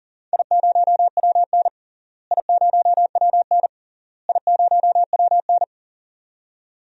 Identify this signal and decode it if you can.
Morse code, heard as a keyed tone, about 40 wpm, 700 Hz. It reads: I0WN I0WN I0WN
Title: DXCC-100 Call Signs - 4 Characters - Repeated 3x - 40wpm